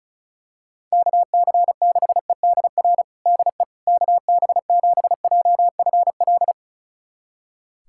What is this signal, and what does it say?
Morse code, heard as a keyed tone, about 36 wpm, 700 Hz. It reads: KC6EDR DE KB7JFL
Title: USA Callsign DE Callsign 35wpm